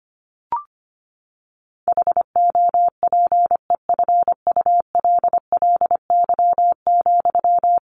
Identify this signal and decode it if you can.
Morse code, heard as a keyed tone, about 25 wpm, 700 Hz. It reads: HOPEFULLY,
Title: Sentences from Top 1500 Words 25wpm